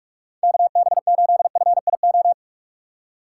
Morse code, heard as a keyed tone, about 46 wpm, 700 Hz. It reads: KB8FIO
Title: USA Call Signs - 6 Characters - 8x Character Spacing - 45wpm